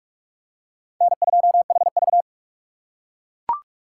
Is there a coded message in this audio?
Yes: Morse code from 1.0 to 2.2 s, heard as a keyed tone, about 39 wpm, 700 Hz. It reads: N2HV